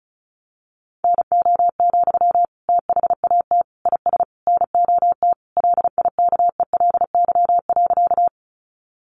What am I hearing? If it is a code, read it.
Morse code, heard as a keyed tone, about 34 wpm, 700 Hz. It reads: NO, THAT IS NOT LIKELY.
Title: Sentences from Top 600 Words 35wpm